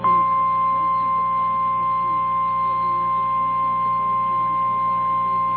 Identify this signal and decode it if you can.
Emergency Alert System (SAME) tones (the 1050 Hz attention tone). It is heard from 0.6 to 5.0 s.
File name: eas / KNCYKBIE / KNCYKBIE